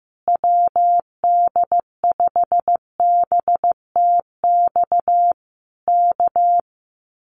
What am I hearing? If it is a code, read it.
Morse code, heard as a keyed tone, about 15 wpm, 700 Hz. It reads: WD5BTX K
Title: Common QSO Phrases Encore 15wpm